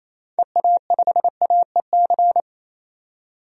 Morse code, heard as a keyed tone, about 27 wpm, 700 Hz. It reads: EA5AEC
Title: DXCC-100 Call Signs - 6 Characters - 28wpm